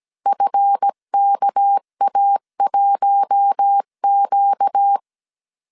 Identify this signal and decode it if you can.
Morse code, heard as a keyed tone, about 17 wpm, 800 Hz. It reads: FKA1Q
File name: CW / DET / 1 9 1500C S B 16wpm 2856